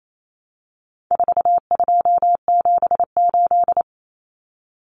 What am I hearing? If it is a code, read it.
Morse code, heard as a keyed tone, about 28 wpm, 700 Hz. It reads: 4278
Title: Numbers - 4-Digit 28wpm